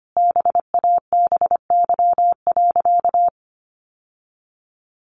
Morse code, heard as a keyed tone, about 25 wpm, 700 Hz. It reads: BABY.
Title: Sentences from Top 1400 Words 25wpm